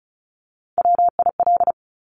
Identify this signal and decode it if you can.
Morse code, heard as a keyed tone, about 36 wpm, 690 Hz. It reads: WIL
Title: Top Three Letter Combinations in Words 35wpm